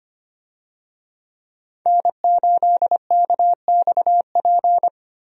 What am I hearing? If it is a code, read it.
Morse code, heard as a keyed tone, about 25 wpm, 700 Hz. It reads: N8KXP
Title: USA Call Signs - 5 Characters - Repeated 3x - 25wpm